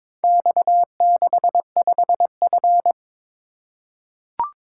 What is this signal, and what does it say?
Morse code, heard as a keyed tone, about 22 wpm, 700 Hz. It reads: X65F
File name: practice / 22wpm / Four Letter-Number Combinations - 22wpm